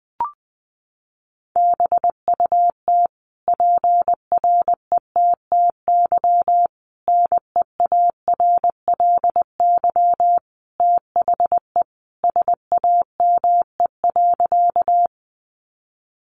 Morse code, heard as a keyed tone, about 20 wpm, 700 Hz. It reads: BUT PRETTY NEARLY THE SAME.
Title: Sentences from Top 1300 Words 20wpm